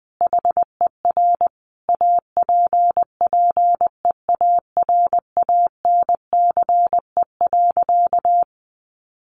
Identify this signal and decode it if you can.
Morse code, heard as a keyed tone, about 19 wpm, 700 Hz. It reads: HER APPEARANCE.
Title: Sentences from Top 1100 Words 20wpm